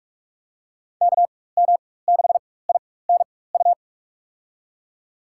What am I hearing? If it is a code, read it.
Morse code, heard as a keyed tone, about 46 wpm, spaced as for 21 wpm, 700 Hz. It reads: KM6INU
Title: USA Call Signs - 6 Characters - 4x Character Spacing - 45wpm